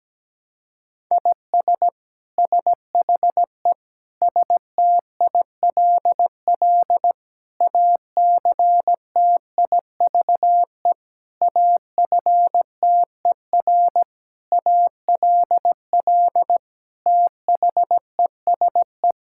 Morse code, heard as a keyed tone, about 17 wpm, 700 Hz. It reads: IS SHE STILL ACTIVE AFTER ALL THESE
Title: Sentences from Top 1000 Words - Review Part 6 - 17wpm